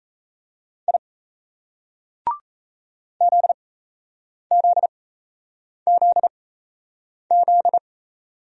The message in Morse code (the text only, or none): I ZZZZ